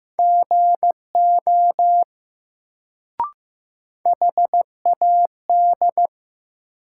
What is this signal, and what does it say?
Morse code, heard as a keyed tone, about 15 wpm, 700 Hz. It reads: GO HAD
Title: Sentences from Top 100 Words - New Word Drill - 15wpm